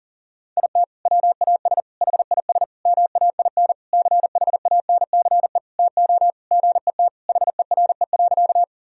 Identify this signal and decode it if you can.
Morse code, heard as a keyed tone, about 37 wpm, 700 Hz. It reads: IT WAS HIS MAIN CHANCE TO GET HERE.
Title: Sentences from Top 1000 Words - Review Part 2 - 40wpm